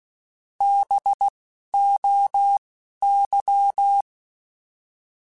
Morse code, heard as a keyed tone, about 16 wpm, spaced as for 11 wpm, 800 Hz. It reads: BOY
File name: Alphabetical / Alphabetical mp3s / Bs @10